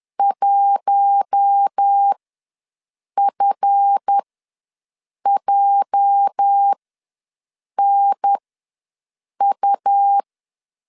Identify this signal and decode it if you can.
Morse code, heard as a keyed tone, about 10 wpm, spaced as for 6 wpm, 800 Hz. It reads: 1FJNU
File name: LEZIONI / 8 YJ45 / 8 YJ ws 10wpm 4043